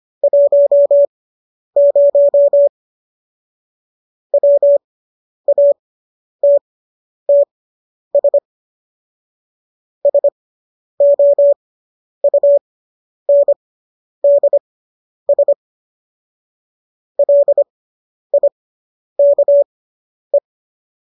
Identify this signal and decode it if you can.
Morse code, heard as a keyed tone, about 25 wpm, spaced as for 10 wpm, 570 Hz. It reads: 10 WATTS SOUNDS LIKE